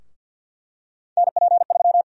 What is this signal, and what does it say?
Morse code, heard as a keyed tone, about 49 wpm, 700 Hz. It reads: NP3
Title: USA Call Sign Prefixes - Repeated 2x - 50wpm